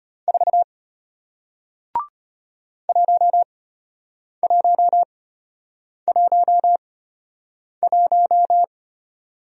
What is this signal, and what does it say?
Morse code, heard as a keyed tone, about 34 wpm, spaced as for 8 wpm, 700 Hz. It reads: 4 1111